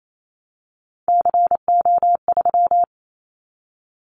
Morse code, heard as a keyed tone, about 28 wpm, 700 Hz. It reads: CO3
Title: Callsign Prefixes for Top 100 DXCC - 28wpm